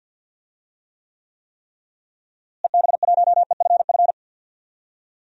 Morse code, heard as a keyed tone, about 52 wpm, 700 Hz. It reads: EB1EFF
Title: DXCC-100 Call Signs - 6 Characters - Repeated 3x - 50wpm